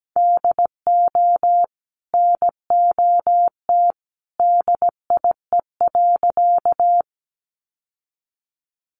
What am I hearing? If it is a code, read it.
Morse code, heard as a keyed tone, about 17 wpm, 700 Hz. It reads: DO NOT DIE.
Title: Sentences from Top 1300 Words 17wpm